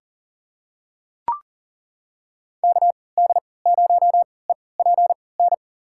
Morse code, heard as a keyed tone, about 40 wpm, spaced as for 23 wpm, 700 Hz. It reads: KD0EPN
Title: USA Call Signs - 6 Characters - 3x Character Spacing - 40wpm